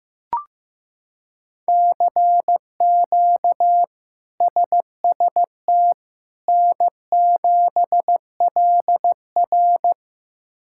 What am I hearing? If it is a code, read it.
Morse code, heard as a keyed tone, about 15 wpm, 700 Hz. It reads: CQ SST N7LR
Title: K1USN SST QSO Practice - 15wpm